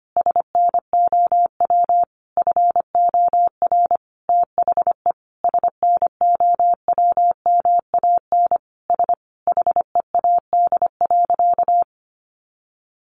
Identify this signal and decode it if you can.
Morse code, heard as a keyed tone, about 25 wpm, 700 Hz. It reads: SNOW FOR THE SNOWMAN S HEAD.
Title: Easy Phrases 25wpm